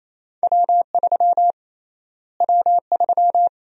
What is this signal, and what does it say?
Morse code, heard as a keyed tone, about 28 wpm, 700 Hz. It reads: W3 W3
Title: USA Call Sign Prefixes - Repeated 2x - 28wpm